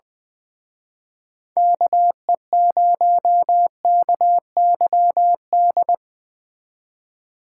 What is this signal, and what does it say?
Morse code, heard as a keyed tone, about 20 wpm, 700 Hz. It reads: KE0KYD